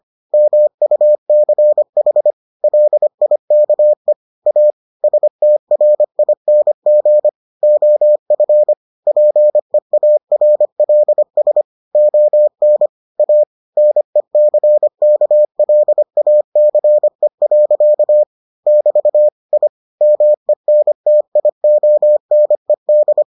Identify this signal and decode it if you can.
Morse code, heard as a keyed tone, about 25 wpm, 600 Hz. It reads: MUCH LIKE A STRING OF PEARLS ON A NECKLACE. <BT> I MENTIONED